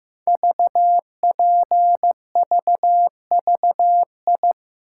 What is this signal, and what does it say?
Morse code, heard as a keyed tone, about 15 wpm, 700 Hz. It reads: VPVVI
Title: Code Groups - 5 Letters - 15wpm